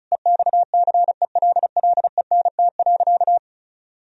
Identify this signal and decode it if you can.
Morse code, heard as a keyed tone, about 36 wpm, 700 Hz. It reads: EXCELLENT.
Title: Sentences from Top 1000 Words - Review Part 4 - 35wpm